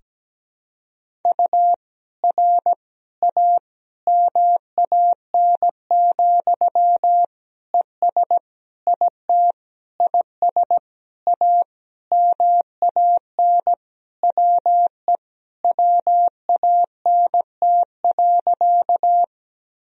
Morse code, heard as a keyed tone, about 17 wpm, 700 Hz. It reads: U R A MAN, ES IT IS A MAN WE WANT.
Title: Sentences from Top 200 Words and Abbreviations 17wpm